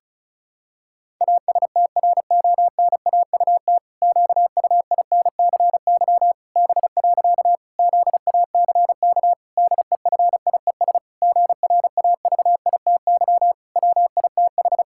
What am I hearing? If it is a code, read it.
Morse code, heard as a keyed tone, about 36 wpm, 700 Hz. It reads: ASTRONAUT QUINCY B. ZACK DEFIES GRAVITY WITH
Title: Pangram Challenge v2 - 35wpm